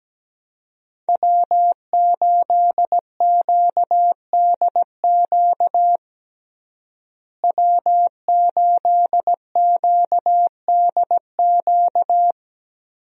Morse code, heard as a keyed tone, about 17 wpm, 700 Hz. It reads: W8QDQ W8QDQ
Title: USA Call Signs - 5 Characters - Repeated 2x - 17wpm